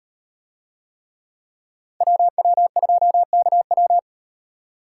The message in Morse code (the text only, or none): WW2KW